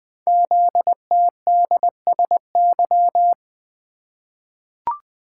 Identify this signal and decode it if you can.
Morse code, heard as a keyed tone, about 20 wpm, 700 Hz. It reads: ZTDSY
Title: Code Groups - 5 Letters - 20wpm